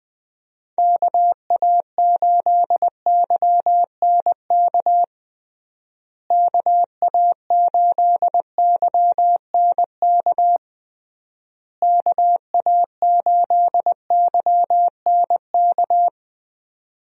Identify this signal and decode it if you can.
Morse code, heard as a keyed tone, about 20 wpm, 700 Hz. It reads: KA8YNK KA8YNK KA8YNK
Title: USA Call Signs - 6 Characters - Repeated 3x - 20wpm